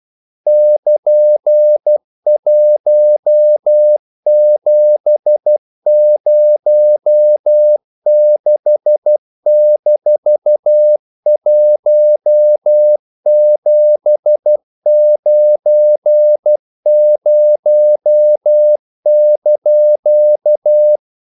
Morse code, heard as a keyed tone, about 12 wpm, 600 Hz. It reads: <KN>1706-1790)
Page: English quotations in CW (telegraphy - morse)